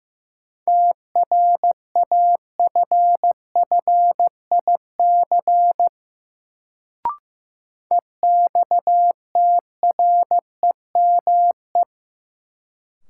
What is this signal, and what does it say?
Morse code, heard as a keyed tone, about 15 wpm, 700 Hz. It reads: TRAFFIC EXTREME
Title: Sentences from Top 1300 Words - New Word Drill Alt - 15wpm